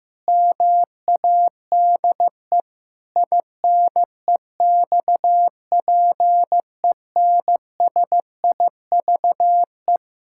Morse code, heard as a keyed tone, about 16 wpm, 700 Hz. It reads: MADE INEXPENSIVE
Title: Pangram Challenge v2 - 15wpm